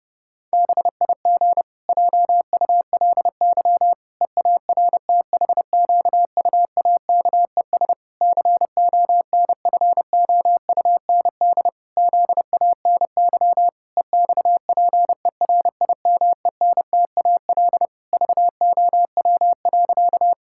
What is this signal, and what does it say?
Morse code, heard as a keyed tone, about 32 wpm, 700 Hz. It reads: BIG JULY EARTHQUAKES CONFOUND ZANY EXPERIMENTAL VOW.
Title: Pangram Challenge v2 - 30wpm